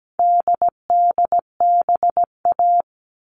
Morse code, heard as a keyed tone, about 17 wpm, 700 Hz. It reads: DDBA
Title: All Cut Numbers - 4-Digit 17wpm